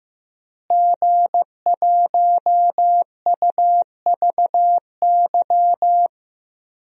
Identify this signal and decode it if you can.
Morse code, heard as a keyed tone, about 15 wpm, 700 Hz. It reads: G1UVY
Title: DXCC-100 Call Signs - 5 Characters - Repeated 2x - 15wpm